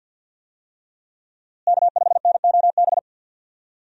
Morse code, heard as a keyed tone, about 49 wpm, 700 Hz. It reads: K5NOB
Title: USA Call Signs - 5 Characters - Repeated 3x - 50wpm